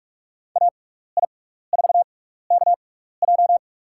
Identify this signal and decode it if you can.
Morse code, heard as a keyed tone, about 44 wpm, spaced as for 15 wpm, 700 Hz. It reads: AI4KJ
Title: USA Call Signs - 5 Characters - 6x Character Spacing - 45wpm